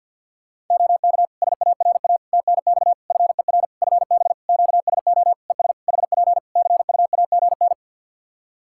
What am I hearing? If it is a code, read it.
Morse code, heard as a keyed tone, about 50 wpm, 700 Hz. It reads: OK SARA TNX FER FB QSO ES HP CUAGN
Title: Practice QSO - Exchange 4 - Ending - 50wpm